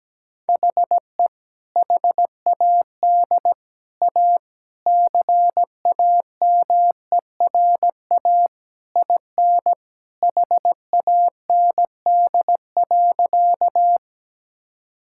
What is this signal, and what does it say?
Morse code, heard as a keyed tone, about 17 wpm, 700 Hz. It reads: HE HAD A CAMERA IN HAND.
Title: Sentences from Top 1000 Words - Review Part 5 - 17wpm